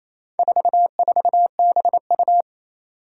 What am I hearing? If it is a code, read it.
Morse code, heard as a keyed tone, about 27 wpm, 700 Hz. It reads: 44BU
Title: Four Challenging Letter-Number Combinations 28wpm